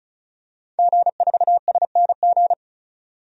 Morse code, heard as a keyed tone, about 36 wpm, 700 Hz. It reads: G4SNG
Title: DXCC-100 Call Signs - 5 Characters - 35wpm